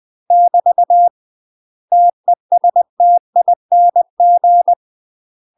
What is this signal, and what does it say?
Morse code, heard as a keyed tone, about 20 wpm, 700 Hz. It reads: <BT> TESTING
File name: UK MASTER QSOs TRAINING / 20WPM / UK MASTER QSOs 0017